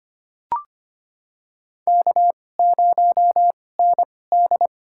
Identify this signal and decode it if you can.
Morse code, heard as a keyed tone, about 25 wpm, spaced as for 18 wpm, 700 Hz. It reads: K0ND
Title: USA Call Signs - 4 Characters - 2x Character Spacing - 25wpm